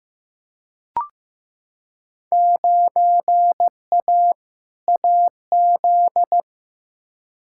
Morse code, heard as a keyed tone, about 15 wpm, 700 Hz. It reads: 9A AZ
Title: Field Day Exchanges - 15wpm